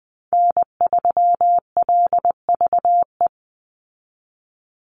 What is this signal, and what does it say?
Morse code, heard as a keyed tone, about 20 wpm, 700 Hz. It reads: N3LVE
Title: USA Call Signs 20wpm